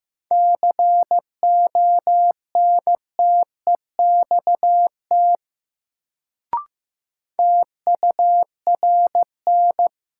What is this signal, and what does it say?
Morse code, heard as a keyed tone, about 15 wpm, 700 Hz. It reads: CONTEXT TURN
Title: Sets of 4 Words - New Word Drill - 15wpm